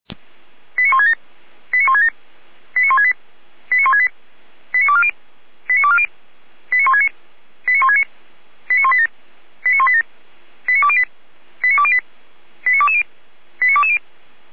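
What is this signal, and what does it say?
Five-tone selective call (ZVEI), heard as quick runs of five tones, tones of about 70 ms. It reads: 89167 89167 89168 89168 89240 89240 89169 89169 89187 89187 89208 89208 89220 89220
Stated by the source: ZVEI five-tone selective call received on an SDR.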